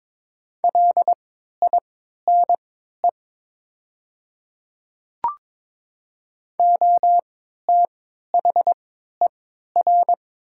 Morse code, heard as a keyed tone, about 22 wpm, spaced as for 12 wpm, 700 Hz. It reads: LINE OTHER
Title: 3x Character Spacing - Top 300 Words - 22wpm